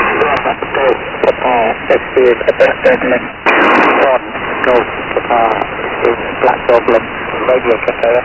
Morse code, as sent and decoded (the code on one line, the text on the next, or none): none